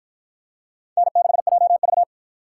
Morse code, heard as a keyed tone, about 53 wpm, 700 Hz. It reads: N6JV